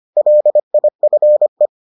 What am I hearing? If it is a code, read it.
Morse code, heard as a keyed tone, about 25 wpm, 600 Hz. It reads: LIFE